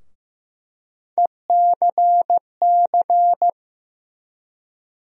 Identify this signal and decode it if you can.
Morse code, heard as a keyed tone, about 15 wpm, 700 Hz. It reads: ECC
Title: USA Call Sign Postfixes - 3 Letters 15wpm